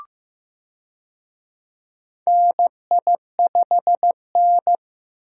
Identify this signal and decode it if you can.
Morse code, heard as a keyed tone, about 16 wpm, 700 Hz. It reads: NI5N
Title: USA Call Signs - 4 Characters - Repeated 2x - 15wpm